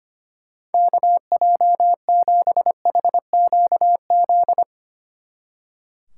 Morse code, heard as a keyed tone, about 31 wpm, 700 Hz. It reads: KJ7HQZ
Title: Speed Racing - USA 2x3 Call Signs 25wpm